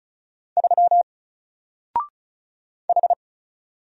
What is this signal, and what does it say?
Morse code, heard as a keyed tone, about 36 wpm, 700 Hz. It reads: 3 H
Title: Single Character (Letter:Number:Punctuation:Prosign) Alt 35wpm